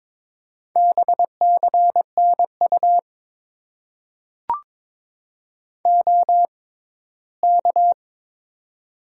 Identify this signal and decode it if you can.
Morse code, heard as a keyed tone, about 22 wpm, spaced as for 15 wpm, 700 Hz. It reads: BCNU OK
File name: Top 90 QSO Elements - 6x Character Spacing - 22wpm